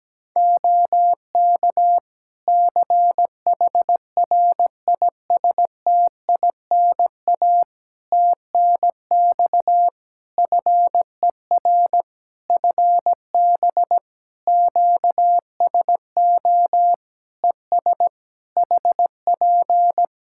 Morse code, heard as a keyed tone, about 17 wpm, 700 Hz. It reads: OK CHRISTINA TNX FER FB QSO ES HP
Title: Practice QSO - Exchange 4 - Ending - 17wpm